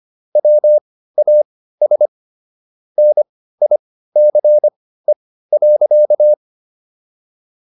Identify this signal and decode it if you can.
Morse code, heard as a keyed tone, about 25 wpm, spaced as for 15 wpm, 600 Hz. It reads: WAS NICE.